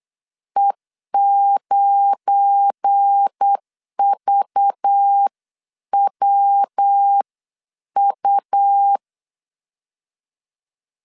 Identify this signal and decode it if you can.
Morse code, heard as a keyed tone, about 8 wpm, 800 Hz. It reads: E9VWU
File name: LEZIONI / 7 GF9Z / 7 GF9 8wpm 3545